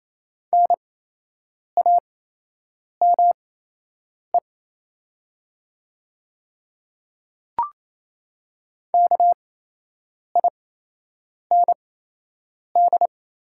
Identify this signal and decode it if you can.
Morse code, heard as a keyed tone, about 28 wpm, spaced as for 8 wpm, 700 Hz. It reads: NAME KIND
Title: 8x Character Spacing - Top 200 Words - 28wpm